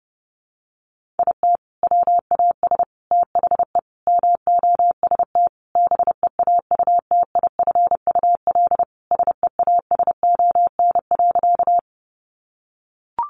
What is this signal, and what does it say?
Morse code, heard as a keyed tone, about 28 wpm, 700 Hz. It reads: IT WAS THE MOST BEAUTIFUL SEASON.
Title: Sentences from Top 1400 Words 30wpm